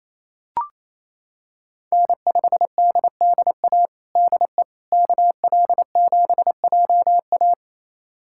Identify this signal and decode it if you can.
Morse code, heard as a keyed tone, about 28 wpm, 700 Hz. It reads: N5DDA DE KL7JA
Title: USA Callsign DE Callsign - 5 Characters - 28wpm